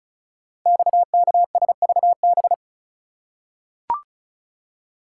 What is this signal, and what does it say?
Morse code, heard as a keyed tone, about 36 wpm, 690 Hz. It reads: XKSVB
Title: Code Groups - 5 Letters - 35wpm